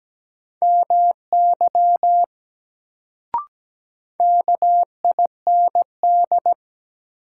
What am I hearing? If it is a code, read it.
Morse code, heard as a keyed tone, about 17 wpm, 700 Hz. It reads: MY KIND
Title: Sentences from Top 200 Words - New Word Drill - 17wpm